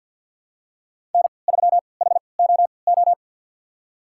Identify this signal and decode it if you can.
Morse code, heard as a keyed tone, about 53 wpm, spaced as for 29 wpm, 700 Hz. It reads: N3HOO
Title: USA Call Signs - 5 Characters - 3x Character Spacing - 50wpm